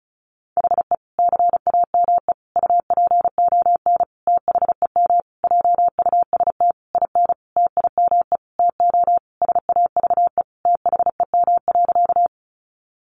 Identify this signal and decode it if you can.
Morse code, heard as a keyed tone, about 34 wpm, 700 Hz. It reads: HE CAME UPON THEM JUST IN TIME TO SAVE THEM.
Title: Sentences from Top 1100 Words 35wpm